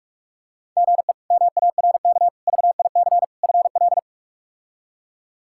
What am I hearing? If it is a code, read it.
Morse code, heard as a keyed tone, about 46 wpm, 700 Hz. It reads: GE MARK VIC FL
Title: K1USN SST QSO Practice - 45wpm